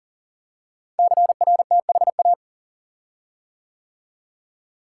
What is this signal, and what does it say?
Morse code, heard as a keyed tone, about 39 wpm, 690 Hz. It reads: CRTHA
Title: Code Groups - 5 Letters - 40wpm